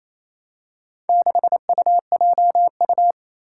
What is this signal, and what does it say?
Morse code, heard as a keyed tone, about 27 wpm, 700 Hz. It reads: 6UJU
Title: Four Challenging Letter-Number Combinations 28wpm